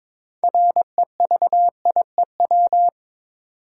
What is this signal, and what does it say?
Morse code, heard as a keyed tone, about 22 wpm, 700 Hz. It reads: REVIEW
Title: Sentences from Top 900 Words - New Word Drill - 22wpm